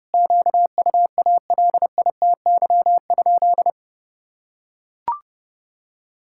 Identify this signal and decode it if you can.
Morse code, heard as a keyed tone, about 30 wpm, 700 Hz. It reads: QUALITY?
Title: Sentences from Top 1000 Words - Review Part 6 - 30wpm